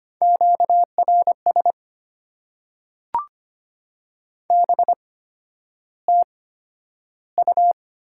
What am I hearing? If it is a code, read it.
Morse code, heard as a keyed tone, about 25 wpm, spaced as for 11 wpm, 690 Hz. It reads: QRS BTU
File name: Top 90 QSO Elements - 8x Character Spacing - 25wpm